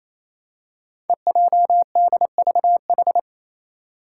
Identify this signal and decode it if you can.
Morse code, heard as a keyed tone, about 28 wpm, 700 Hz. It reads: EJDVH
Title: Five Challenging Letter-Number Combinations 28wpm